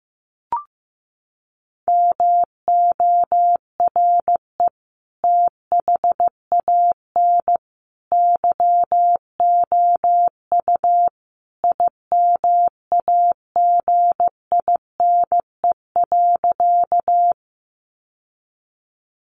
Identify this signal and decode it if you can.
Morse code, heard as a keyed tone, about 15 wpm, 700 Hz. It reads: MORE THAN YOU IMAGINE.
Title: Sentences from Top 1400 Words 15wpm